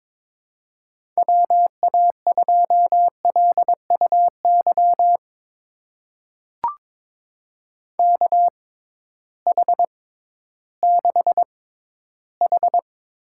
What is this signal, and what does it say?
Morse code, heard as a keyed tone, about 22 wpm, spaced as for 13 wpm, 700 Hz. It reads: WA2LUY KH6H
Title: USA Call Signs - 6 Characters - 6x Character Spacing - 22wpm